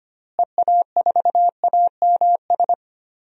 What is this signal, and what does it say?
Morse code, heard as a keyed tone, about 25 wpm, 700 Hz. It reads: EA4AMS
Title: DXCC-100 Call Signs - 6 Characters - Repeated 3x - 25wpm